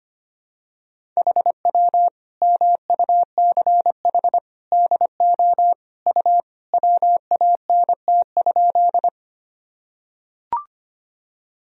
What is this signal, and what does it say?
Morse code, heard as a keyed tone, about 25 wpm, 700 Hz. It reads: HW MUCH DO U WANT?
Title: Sentences from Top 200 Words and Abbreviations 25wpm